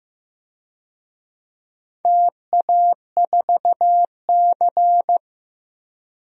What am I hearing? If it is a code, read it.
Morse code, heard as a keyed tone, about 15 wpm, 700 Hz. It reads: TA4C
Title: DXCC-100 Call Signs - 4 Characters - 15wpm